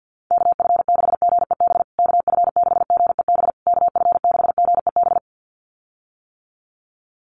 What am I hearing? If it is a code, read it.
Morse code, heard as a keyed tone, about 48 wpm, 700 Hz. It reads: KF6GEB KF6GEB KF6GEB
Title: USA Call Signs Repeated 3x 50wpm